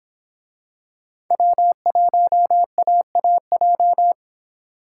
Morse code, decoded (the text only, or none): W1AAJ